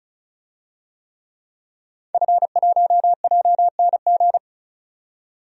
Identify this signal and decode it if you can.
Morse code, heard as a keyed tone, about 34 wpm, 700 Hz. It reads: F1JNG
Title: DXCC-100 Call Signs - 5 Characters - 35wpm